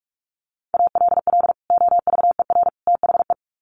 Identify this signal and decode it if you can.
Morse code, heard as a keyed tone, about 44 wpm, 700 Hz. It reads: ALL OVER THE
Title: Sets of 3 Words 45wpm